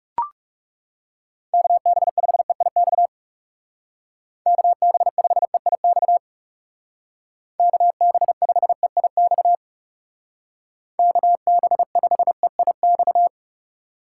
Morse code, heard as a keyed tone, about 36 wpm, 700 Hz. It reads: KB5EIX KB5EIX KB5EIX KB5EIX
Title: Speed Racing - USA 2x3 Call Signs 30wpm